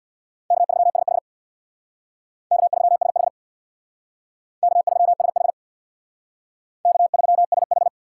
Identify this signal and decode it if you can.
Morse code, heard as a keyed tone, about 64 wpm, 700 Hz. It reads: K3SH K3SH K3SH K3SH